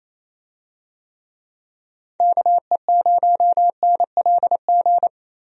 Morse code, heard as a keyed tone, about 27 wpm, 700 Hz. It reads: KE0NLG
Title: USA Call Signs - 6 Characters - 28wpm